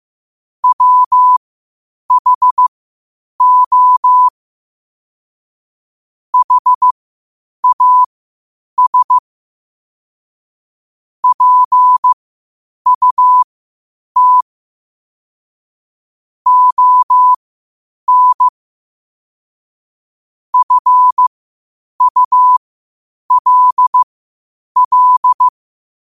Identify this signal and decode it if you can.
Morse code, heard as a keyed tone, about 15 wpm, spaced as for 8 wpm, 1000 Hz. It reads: WHO HAS PUT ON FULL
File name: qotdfiles / 8wpm